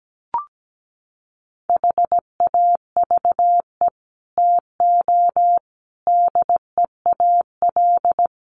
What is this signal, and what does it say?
Morse code, heard as a keyed tone, about 17 wpm, 700 Hz. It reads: HAVE TO DEAL
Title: Sets of 5 Words Encore2 17wpm